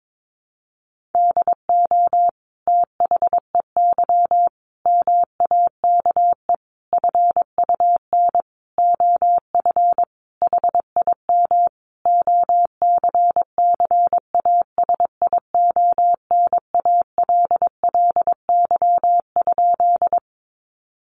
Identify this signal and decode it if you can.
Morse code, heard as a keyed tone, about 22 wpm, 700 Hz. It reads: DO THEY MAKE FUN OF HIM OCCASIONALLY?
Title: Sentences from Top 1300 Words 22wpm